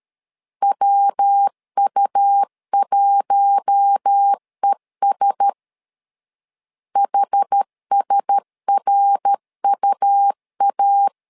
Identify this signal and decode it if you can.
Morse code, heard as a keyed tone, about 13 wpm, 800 Hz. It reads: WU1ES HSRUA
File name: LEZIONI / 4 Q1S 12wpm 1802